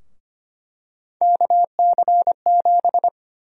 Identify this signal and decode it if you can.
Morse code, heard as a keyed tone, about 25 wpm, 700 Hz. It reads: KC7
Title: USA Call Sign Prefixes - Repeated 2x - 25wpm